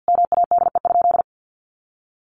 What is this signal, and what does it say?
Morse code, heard as a keyed tone, about 50 wpm, 700 Hz. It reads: MADE?
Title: Sentences from Top 1200 Words 50wpm